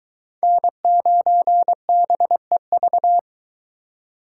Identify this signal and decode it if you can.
Morse code, heard as a keyed tone, about 22 wpm, 700 Hz. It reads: N9BEV